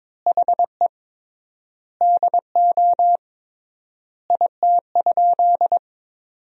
Morse code, heard as a keyed tone, about 22 wpm, 700 Hz. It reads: HE DO IT?
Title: 3x Word-Spacing - Sentences from Top 100 Words - 22wpm